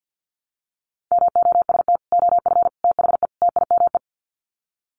Morse code, heard as a keyed tone, about 50 wpm, 700 Hz. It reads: MOST OF THE TIME
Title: Sets of 4 Words 50wpm